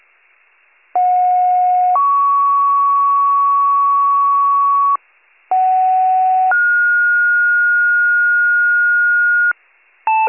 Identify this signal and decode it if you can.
Two-tone page, heard as a one-second tone, then a three-second tone, about 730 then 1120 Hz, about 750 then 1430 Hz, with noise. It is heard from 0.9 to 9.5 s.